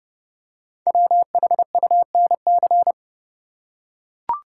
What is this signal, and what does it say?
Morse code, heard as a keyed tone, about 29 wpm, 700 Hz. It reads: WHUNC
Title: Code Groups - 5 Letters - 30wpm